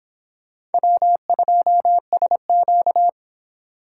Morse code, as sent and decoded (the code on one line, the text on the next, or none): .-- ..--- ... --.-
W2SQ